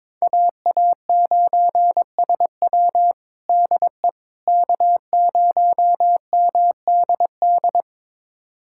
Morse code, heard as a keyed tone, about 21 wpm, 700 Hz. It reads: AA9SW DE K0MDD
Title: USA Callsign DE Callsign 22wpm